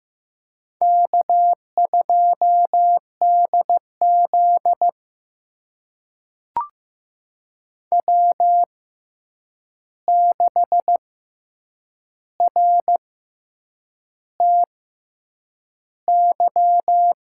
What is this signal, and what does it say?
Morse code, heard as a keyed tone, about 15 wpm, spaced as for 7 wpm, 700 Hz. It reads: K2DZ W6RTY
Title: USA Call Signs - 6x Character Spacing - 15wpm